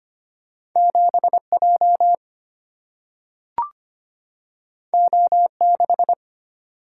Morse code, heard as a keyed tone, about 25 wpm, 700 Hz. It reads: 7 J O 6
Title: Single-Letter Single-Number 25wpm